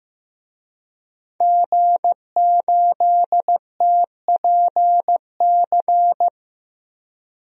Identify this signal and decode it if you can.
Morse code, heard as a keyed tone, about 15 wpm, 700 Hz. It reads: G8TPC
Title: DXCC-100 Call Signs - 5 Characters - Repeated 3x - 15wpm